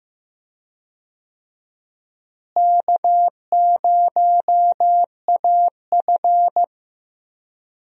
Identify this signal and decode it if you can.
Morse code, heard as a keyed tone, about 15 wpm, 700 Hz. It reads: K0AF
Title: USA Call Signs - 4 Characters - Repeated 2x - 15wpm